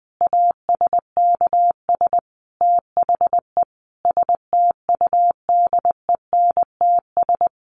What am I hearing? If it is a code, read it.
Morse code, heard as a keyed tone, about 20 wpm, 700 Hz. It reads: ASKS THE STUDENTS
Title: Additional Easy Phrases 20wpm